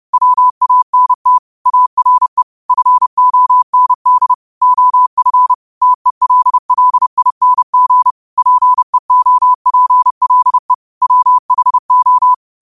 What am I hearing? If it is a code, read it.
Morse code, heard as a keyed tone, about 30 wpm, 1000 Hz. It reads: WANT ARE FOND OF TELLING PEOPLE WHO